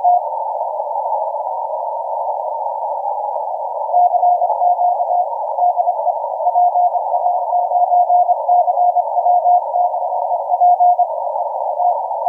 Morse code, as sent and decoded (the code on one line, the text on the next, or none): none